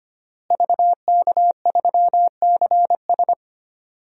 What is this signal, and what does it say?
Morse code, heard as a keyed tone, about 25 wpm, 700 Hz. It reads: VK3CS
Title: DXCC-100 Call Signs - 5 Characters - Repeated 2x - 25wpm